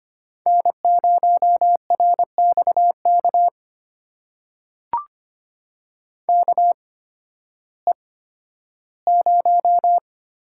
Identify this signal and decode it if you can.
Morse code, heard as a keyed tone, about 25 wpm, spaced as for 13 wpm, 700 Hz. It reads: N0RXK KE0
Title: USA Call Signs - 8x Character Spacing - 25wpm